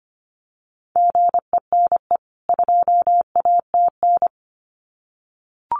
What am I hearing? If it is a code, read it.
Morse code, heard as a keyed tone, about 25 wpm, 700 Hz. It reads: GENE 2ATN
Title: CWT Exchange 25wpm